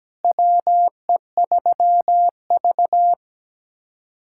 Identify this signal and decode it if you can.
Morse code, heard as a keyed tone, about 17 wpm, 700 Hz. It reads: WE3V
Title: USA Callsign DE Callsign - 4 Characters - 17wpm